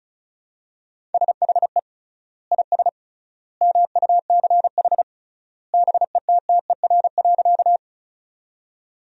Morse code, heard as a keyed tone, about 36 wpm, 700 Hz. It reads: SHE IS MUCH BETTER.
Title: 3x Word-Spacing - Sentences from Top 400 Words - 35wpm